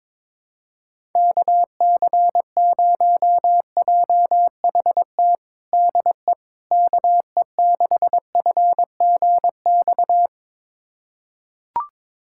Morse code, heard as a keyed tone, about 22 wpm, 700 Hz. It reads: KC0JHT DE KE6FGX
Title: USA Callsign DE Callsign - 6 Characters - 22wpm